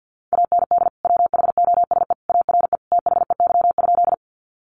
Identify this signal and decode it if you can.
Morse code, heard as a keyed tone, about 50 wpm, 700 Hz. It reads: AND WHOSE ARE THEY?
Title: Sentences from Top 1200 Words 50wpm